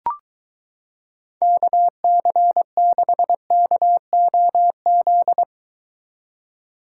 Morse code, heard as a keyed tone, about 23 wpm, 700 Hz. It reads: KC6KOZ